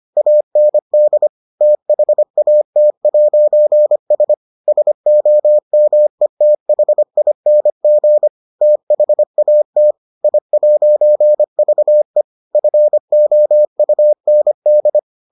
Morse code, heard as a keyed tone, about 25 wpm, 600 Hz. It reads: AND THAT'S SOMETHING THAT I'VE FOUND